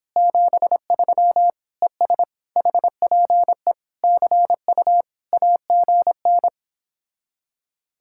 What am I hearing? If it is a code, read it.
Morse code, heard as a keyed tone, about 29 wpm, 700 Hz. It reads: 73 ES HPE CU AGN